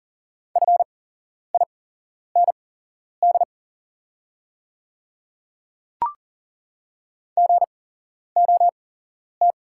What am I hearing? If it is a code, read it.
Morse code, heard as a keyed tone, about 40 wpm, spaced as for 11 wpm, 690 Hz. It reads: FIND GOT
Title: 8x Character Spacing - Top 300 Words - 40wpm